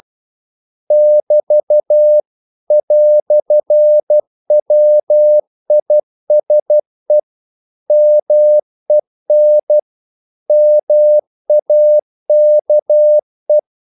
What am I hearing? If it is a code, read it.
Morse code, heard as a keyed tone, about 12 wpm, 600 Hz. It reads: <BT> "WISE MEN MAKE
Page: English quotations in CW (telegraphy - morse)